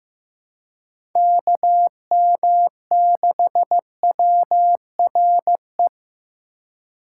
Morse code, heard as a keyed tone, about 18 wpm, 700 Hz. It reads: KM6WRE
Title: Speed Racing - USA 2x3 Call Signs 15wpm